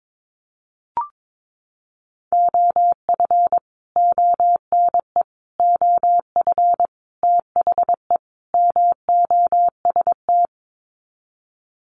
Morse code, heard as a keyed tone, about 22 wpm, 700 Hz. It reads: OF ONE OF THE MOST
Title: Sets of 5 Words Encore2 22wpm